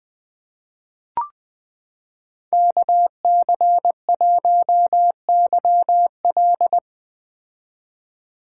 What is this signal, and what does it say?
Morse code, heard as a keyed tone, about 20 wpm, 700 Hz. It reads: KC1YL
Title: 2023 POTA Activator Call Signs - 20wpm